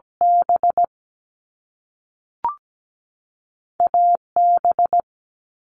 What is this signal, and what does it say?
Morse code, heard as a keyed tone, about 17 wpm, spaced as for 5 wpm, 700 Hz. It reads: BAB
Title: Canadian Provinces and Territories 17wpm